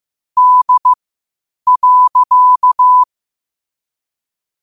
Morse code, heard as a keyed tone, about 15 wpm, spaced as for 8 wpm, 1000 Hz. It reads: D.
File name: qotdfiles / 8wpm